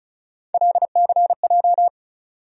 Morse code, heard as a keyed tone, about 36 wpm, 700 Hz. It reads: LCJ
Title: USA Call Sign Postfixes - 3 Letters 35wpm